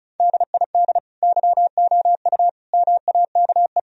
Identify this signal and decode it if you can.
Morse code, heard as a keyed tone, about 36 wpm, 700 Hz. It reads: DID YOU MAKE
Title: Sentences from Top 1000 Words - Review Part 4 - 35wpm